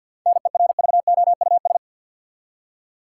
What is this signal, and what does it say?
Morse code, heard as a keyed tone, about 53 wpm, 700 Hz. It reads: NERVOUS
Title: Seven Letter Words - 50wpm